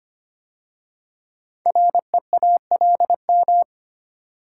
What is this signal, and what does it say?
Morse code, heard as a keyed tone, about 25 wpm, 700 Hz. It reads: REALM
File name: 5 Letter Words - First 3 Letters Given - 25wpm